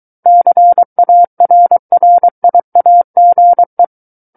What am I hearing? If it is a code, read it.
Morse code, heard as a keyed tone, about 23 wpm, 700 Hz. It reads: CARRIAGE
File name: Book One / 14-Chapter 14 - 23 WPM